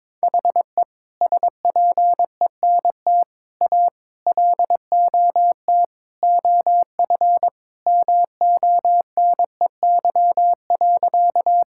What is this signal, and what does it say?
Morse code, heard as a keyed tone, about 21 wpm, 700 Hz. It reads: HE SPENT A LOT OF MONEY.
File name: practice / 22wpm / Sentences with 4-Word Phrases - 22wpm